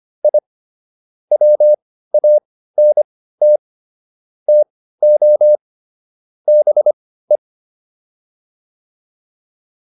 Morse code, heard as a keyed tone, about 25 wpm, spaced as for 15 wpm, 600 Hz. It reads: I WANT TO BE
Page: intermediate-practice-files